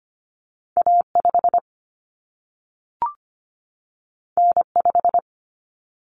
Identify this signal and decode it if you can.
Morse code, heard as a keyed tone, about 25 wpm, 700 Hz. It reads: A5 N5
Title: Common Cut Numbers - 2-Digit 25wpm